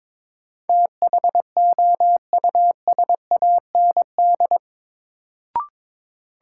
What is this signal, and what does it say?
Morse code, heard as a keyed tone, about 21 wpm, 700 Hz. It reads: THOUSAND
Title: Sentences from Top 600 Words - New Word Drill - 22wpm